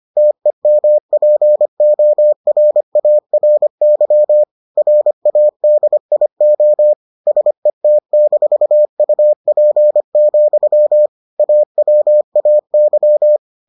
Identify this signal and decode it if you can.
Morse code, heard as a keyed tone, about 25 wpm, 600 Hz. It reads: TEMPORARY RADIO SET-UP, AWAY